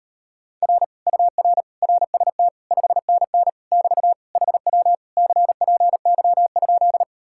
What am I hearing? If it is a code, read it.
Morse code, heard as a keyed tone, about 38 wpm, 700 Hz. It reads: R UR RST 5NN <BT> HW CPY?